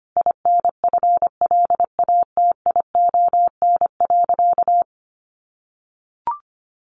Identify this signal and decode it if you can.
Morse code, heard as a keyed tone, about 25 wpm, 700 Hz. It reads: INFLATION.
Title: Sentences from Top 1100 Words 25wpm